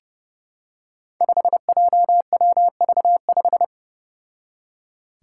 Morse code, heard as a keyed tone, about 31 wpm, 700 Hz. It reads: 5JWV5
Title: Five Challenging Letter-Number Combinations 30wpm